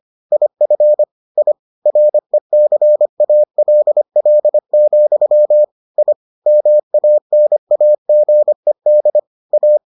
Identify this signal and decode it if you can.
Morse code, heard as a keyed tone, about 25 wpm, 600 Hz. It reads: IF I RECALL, I MANAGED A